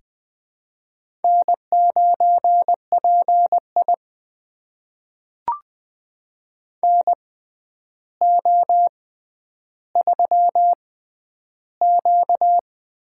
Morse code, heard as a keyed tone, about 20 wpm, spaced as for 10 wpm, 700 Hz. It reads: N9PI NO3Q
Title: USA Call Signs - 6x Character Spacing - 20wpm